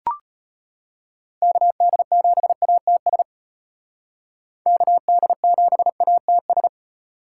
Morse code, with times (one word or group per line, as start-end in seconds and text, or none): 1.4-3.2 KD7ATS
4.7-6.7 KD7ATS